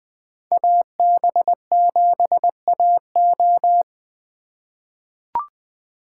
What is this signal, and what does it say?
Morse code, heard as a keyed tone, about 20 wpm, 700 Hz. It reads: AB7AO
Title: USA Call Signs - 5 Characters - 3x Character Spacing - 20wpm